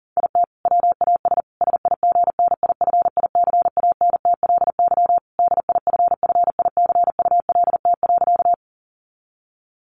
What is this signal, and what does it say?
Morse code, heard as a keyed tone, about 40 wpm, 700 Hz. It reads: IT WAS SIGNIFICANTLY DIFFICULT.
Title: Sentences from Top 1500 Words 40wpm